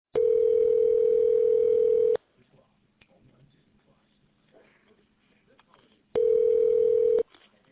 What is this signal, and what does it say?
Ringback tone (North America), 440 + 480 Hz.